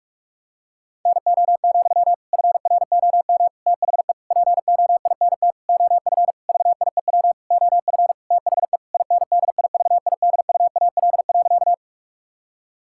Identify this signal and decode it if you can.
Morse code, heard as a keyed tone, about 46 wpm, 700 Hz. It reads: NO, FROM THE POINT OF VIEW OF THE INDIVIDUAL.
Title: Sentences from Top 1000 Words - Review Part 6 - 45wpm